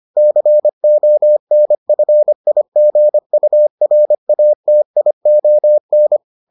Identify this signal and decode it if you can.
Morse code, heard as a keyed tone, about 25 wpm, 600 Hz. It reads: CONFIGURATION